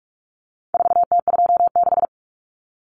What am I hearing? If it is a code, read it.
Morse code, heard as a keyed tone, about 44 wpm, 700 Hz. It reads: 4T26
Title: Common Cut Numbers - 4-Digit 45wpm